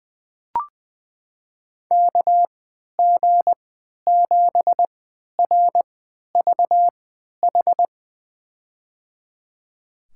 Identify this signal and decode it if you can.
Morse code, heard as a keyed tone, about 21 wpm, spaced as for 11 wpm, 700 Hz. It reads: KG7RVH
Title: USA Call Signs - 6 Characters - 3x Character Spacing - 20wpm